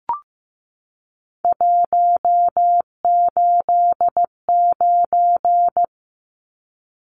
Morse code, heard as a keyed tone, about 15 wpm, 700 Hz. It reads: 189
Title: Numbers - 3-Digit 15wpm